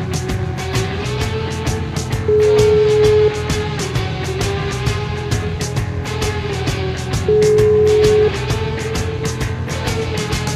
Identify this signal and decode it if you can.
Ringback tone (Europe, 425 Hz), with noise.